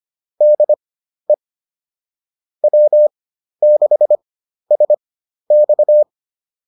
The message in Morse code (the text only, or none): DE W6SX